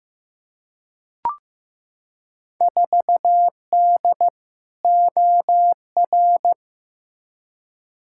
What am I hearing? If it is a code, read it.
Morse code, heard as a keyed tone, about 15 wpm, 700 Hz. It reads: 4D OR
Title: Field Day Exchanges Encore - 15wpm